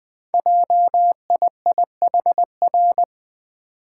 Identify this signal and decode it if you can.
Morse code, heard as a keyed tone, about 21 wpm, 700 Hz. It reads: JIIHR
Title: Code Groups - 5 Letters - 20wpm